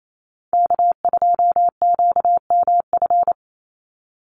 Morse code, heard as a keyed tone, about 28 wpm, 700 Hz. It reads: K2QMF
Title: Call Signs Repeated 2x - 28wpm